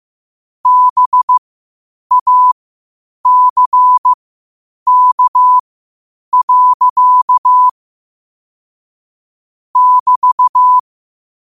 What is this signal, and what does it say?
Morse code, heard as a keyed tone, about 15 wpm, spaced as for 8 wpm, 1000 Hz. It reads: BACK. <BT>
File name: qotdfiles / 8wpm